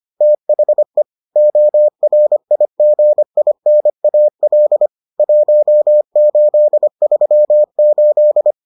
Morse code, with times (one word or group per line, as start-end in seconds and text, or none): 0.2-1.0 THE
1.4-4.9 ORIGINAL
5.2-8.5 1838